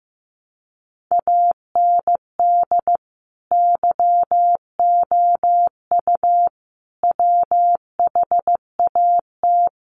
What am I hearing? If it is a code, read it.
Morse code, heard as a keyed tone, about 15 wpm, 700 Hz. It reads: AND YOU WHAT
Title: Sentences from Top 600 Words 15wpm